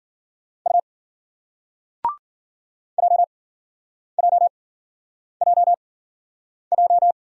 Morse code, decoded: U JJJJ

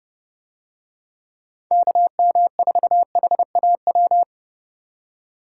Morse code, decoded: KM4HAW

